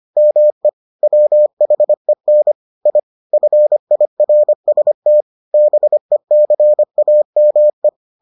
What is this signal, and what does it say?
Morse code, heard as a keyed tone, about 25 wpm, 600 Hz. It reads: ME WHEN I FIRST BECAME